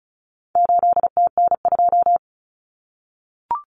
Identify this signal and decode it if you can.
Morse code, heard as a keyed tone, about 34 wpm, 700 Hz. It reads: 8TN2
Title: Common Cut Numbers - 4-Digit 35wpm